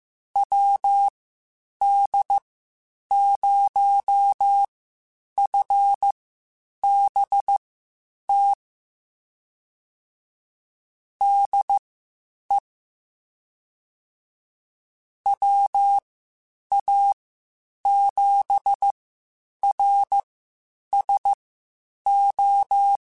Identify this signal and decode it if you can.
Morse code, heard as a keyed tone, about 15 wpm, spaced as for 8 wpm, 800 Hz. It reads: WD0FBT DE WA7RSO
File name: (spaced) / QSOs Spaced-mp3s / WD0FBT 1 @13 sp